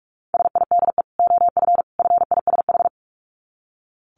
Morse code, heard as a keyed tone, about 44 wpm, 700 Hz. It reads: SIDE OF FISH
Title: Dit Challenge 45wpm